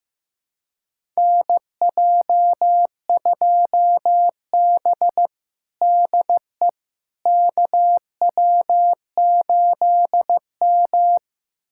Morse code, heard as a keyed tone, about 15 wpm, 700 Hz. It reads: NJ2B DE KW8M
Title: USA Callsign DE Callsign - 4 Characters - 15wpm